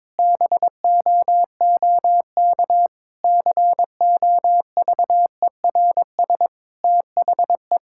Morse code, heard as a keyed tone, about 22 wpm, 700 Hz. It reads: BOOK COVERS THE
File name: practice / 22wpm / Sentences with 4-Word Phrases - 22wpm